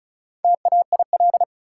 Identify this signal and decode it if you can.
Morse code, heard as a keyed tone, about 36 wpm, 700 Hz. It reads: TAIL
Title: Sentences from Top 1000 Words - Review Part 3 - New Word Drill - 35wpm